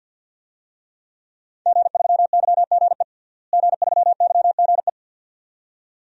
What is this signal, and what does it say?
Morse code, heard as a keyed tone, about 51 wpm, 700 Hz. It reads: G3YGE G3YGE
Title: DXCC-100 Call Signs - 5 Characters - Repeated 2x - 50wpm